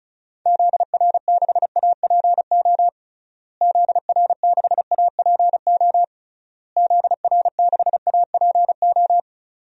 Morse code, heard as a keyed tone, about 34 wpm, 700 Hz. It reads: ZR6APO ZR6APO ZR6APO
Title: DXCC-100 Call Signs - 6 Characters - Repeated 3x - 35wpm